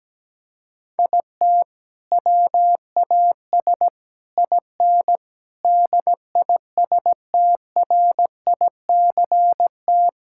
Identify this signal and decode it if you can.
Morse code, heard as a keyed tone, about 17 wpm, 700 Hz. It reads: IT WAS IN DISTRICT
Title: Sentences from Top 1600 Words 17wpm